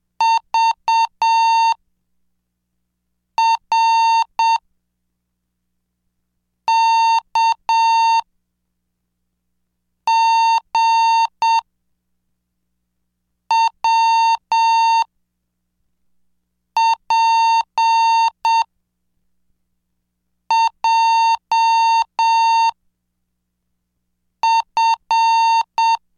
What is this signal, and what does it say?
Morse code, heard as a keyed tone, about 7 wpm, 910 Hz. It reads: V R K G W P J F